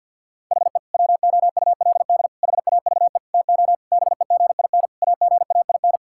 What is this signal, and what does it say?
Morse code, heard as a keyed tone, about 50 wpm, 700 Hz. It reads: HE WOULD HAVE TO BEGIN AGAIN
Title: Sentences from Top 1000 Words - Review Part 5 - 50wpm